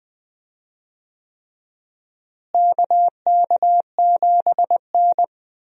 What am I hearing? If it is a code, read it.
Morse code, heard as a keyed tone, about 21 wpm, 700 Hz. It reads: KK7N
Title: USA Call Signs - 4 Characters - Repeated 3x - 20wpm